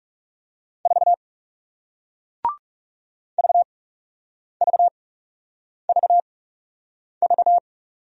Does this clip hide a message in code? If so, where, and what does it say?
Yes: Morse code from 0.9 to 7.6 s, heard as a keyed tone, about 41 wpm, spaced as for 8 wpm, 700 Hz. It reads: 4 VVVV